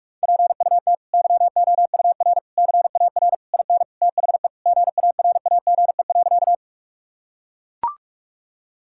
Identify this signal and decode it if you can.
Morse code, heard as a keyed tone, about 43 wpm, 700 Hz. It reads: PUT YOUR CAR IN THE GARAGE.
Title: Sentences from Top 1600 Words 45wpm